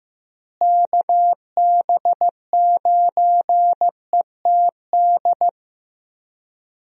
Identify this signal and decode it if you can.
Morse code, heard as a keyed tone, about 16 wpm, 700 Hz. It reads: KB9ETD
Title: USA Call Signs - 6 Characters - Repeated 3x - 15wpm